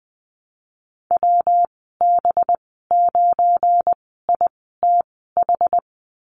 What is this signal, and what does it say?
Morse code, heard as a keyed tone, about 20 wpm, spaced as for 14 wpm, 700 Hz. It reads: WB9ITH
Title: 186 Lesson 30g - Letter b - Call Signs 20wpm